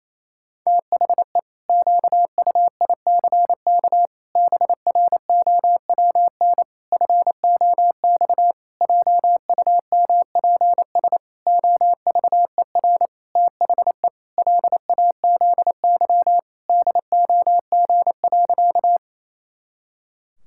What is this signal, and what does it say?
Morse code, heard as a keyed tone, about 28 wpm, 700 Hz. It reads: THE QUICK BROWN FOX JUMPS OVER THE LAZY DOG.
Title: Pangram Challenge v2 - 28wpm